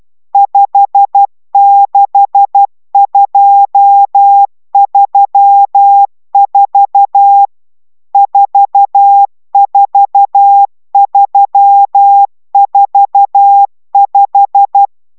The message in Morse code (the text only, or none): 56234 44345